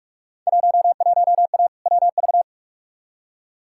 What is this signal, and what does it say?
Morse code, heard as a keyed tone, about 46 wpm, 700 Hz. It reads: 11A WV
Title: Field Day Exchanges Encore - 45wpm